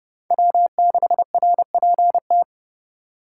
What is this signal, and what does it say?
Morse code, heard as a keyed tone, about 28 wpm, 700 Hz. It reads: W6RPT